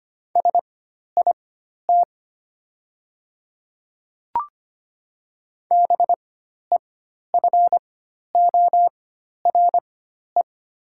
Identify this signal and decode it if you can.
Morse code, heard as a keyed tone, about 25 wpm, spaced as for 12 wpm, 690 Hz. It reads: SIT BEFORE
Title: 4x Character Spacing - Top 500 Words - 25wpm